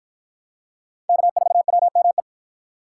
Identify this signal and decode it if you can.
Morse code, heard as a keyed tone, about 53 wpm, 700 Hz. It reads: K4WME